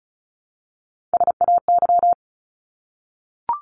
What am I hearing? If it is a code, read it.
Morse code, heard as a keyed tone, about 34 wpm, 700 Hz. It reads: SAY
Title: ES:S Suffix - Top 100 Words with Base Words 35wpm